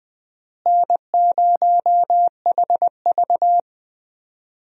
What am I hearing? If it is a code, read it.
Morse code, heard as a keyed tone, about 20 wpm, 700 Hz. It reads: N0HV
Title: USA Call Signs - 4 Characters - Repeated 3x - 20wpm